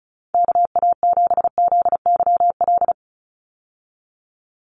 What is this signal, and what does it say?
Morse code, heard as a keyed tone, about 34 wpm, 700 Hz. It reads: KA7ZYL
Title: USA Call Signs 35wpm